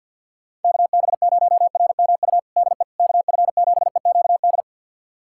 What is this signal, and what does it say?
Morse code, heard as a keyed tone, about 51 wpm, 700 Hz. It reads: KB0RMU DE KF7EQD
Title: USA Callsign DE Callsign - 6 Characters - 50wpm